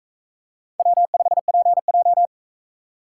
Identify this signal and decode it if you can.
Morse code, heard as a keyed tone, about 40 wpm, 700 Hz. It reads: W5PJ